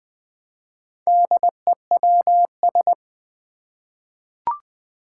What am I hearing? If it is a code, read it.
Morse code, heard as a keyed tone, about 20 wpm, 700 Hz. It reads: DEWS
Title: Four Challenging Letter-Number Combinations 20wpm